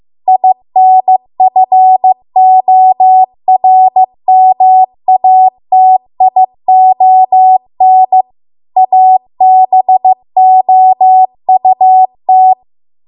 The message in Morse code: INFORMATION ABOUT

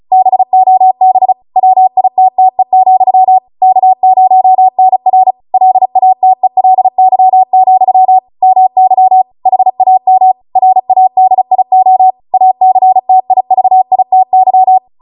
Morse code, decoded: BOB WITTE, K0NR LATELY, MY HAM RADIO ACTIVITY

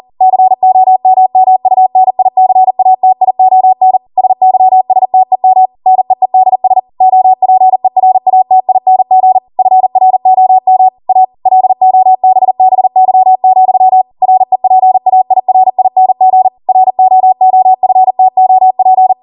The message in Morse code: COMMUNICATION SYSTEM NEEDS OPERATING FROM A LOBBY, REPAIRING ROOFTOP

